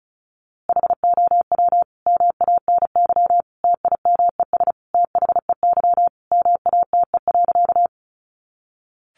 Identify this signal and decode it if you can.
Morse code, heard as a keyed tone, about 34 wpm, 700 Hz. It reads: HOW MANY TIMES THEY MATE.
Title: A Sound Of Thunder 35wpm